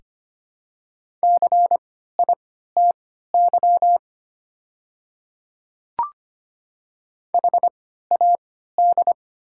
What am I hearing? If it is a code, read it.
Morse code, heard as a keyed tone, about 25 wpm, spaced as for 14 wpm, 700 Hz. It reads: CITY HAD
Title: 3x Character Spacing - Top 300 Words - 25wpm